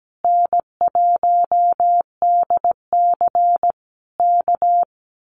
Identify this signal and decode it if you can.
Morse code, heard as a keyed tone, about 17 wpm, 700 Hz. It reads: N1DC K
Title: Common QSO Phrases Encore 17wpm